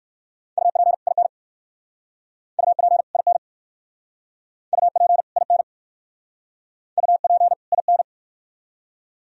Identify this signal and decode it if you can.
Morse code, heard as a keyed tone, about 68 wpm, 700 Hz. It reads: UP IN UP IN UP IN UP IN